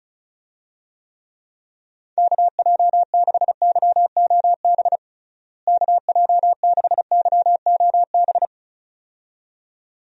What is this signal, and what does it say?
Morse code, heard as a keyed tone, about 34 wpm, 700 Hz. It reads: KJ6YOB KJ6YOB
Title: USA Call Signs - 6 Characters - Repeated 2x - 35wpm